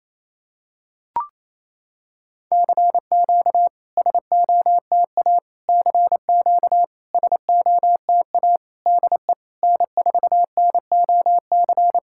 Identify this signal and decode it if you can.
Morse code, heard as a keyed tone, about 28 wpm, 700 Hz. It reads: CQ SOTA CQ SOTA DE N4NOC
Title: SOTA QSO Practice 28wpm